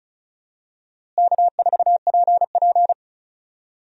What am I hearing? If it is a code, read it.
Morse code, heard as a keyed tone, about 36 wpm, 700 Hz. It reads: K4PP
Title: USA Call Signs - 4 Characters - Repeated 3x - 35wpm